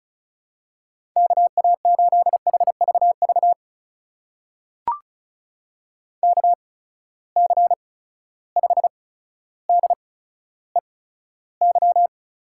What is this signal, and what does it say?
Morse code, heard as a keyed tone, about 36 wpm, spaced as for 15 wpm, 700 Hz. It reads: KA8HVV KC5DEY
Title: USA Call Signs - 8x Character Spacing - 35wpm